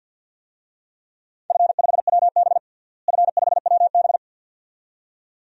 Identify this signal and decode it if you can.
Morse code, heard as a keyed tone, about 46 wpm, 700 Hz. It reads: F5WB F5WB
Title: DXCC-100 Call Signs - 4 Characters - Repeated 2x - 50wpm